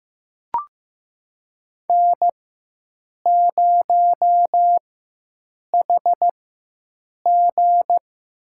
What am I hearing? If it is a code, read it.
Morse code, heard as a keyed tone, about 15 wpm, spaced as for 7 wpm, 700 Hz. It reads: N0HG
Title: USA Call Signs - 4x Character Spacing - 15wpm